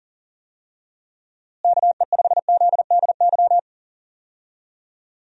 Morse code, heard as a keyed tone, about 40 wpm, 700 Hz. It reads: KE5ZDY
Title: USA Call Signs - 6 Characters - Repeated 3x - 40wpm